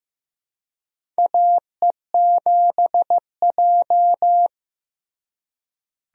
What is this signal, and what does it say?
Morse code, heard as a keyed tone, about 15 wpm, 700 Hz. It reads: AE7J
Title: USA Call Signs - 4 Characters - Repeated 2x - 15wpm